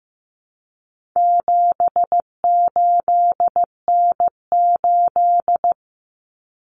Morse code, heard as a keyed tone, about 15 wpm, 700 Hz. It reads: 78N8
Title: Common Cut Numbers - 4-Digit 15wpm